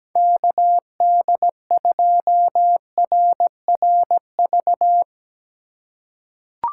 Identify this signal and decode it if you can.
Morse code, heard as a keyed tone, about 17 wpm, 700 Hz. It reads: KD2RRV
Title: USA Call Signs - 6 Characters - 4x Character Spacing - 17wpm